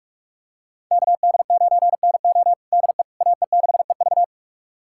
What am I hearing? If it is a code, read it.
Morse code, heard as a keyed tone, about 43 wpm, 700 Hz. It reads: KD9NO DE AE6EV
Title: USA Callsign DE Callsign - 5 Characters - 45wpm